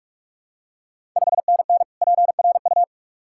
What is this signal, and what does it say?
Morse code, heard as a keyed tone, about 46 wpm, 700 Hz. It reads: 5NN PRU
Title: Ontario QSO Party Exchange 45wpm